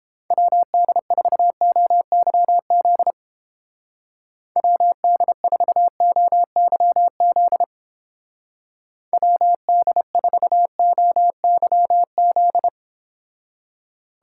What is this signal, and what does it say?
Morse code, heard as a keyed tone, about 30 wpm, 700 Hz. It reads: WD4OYZ WD4OYZ WD4OYZ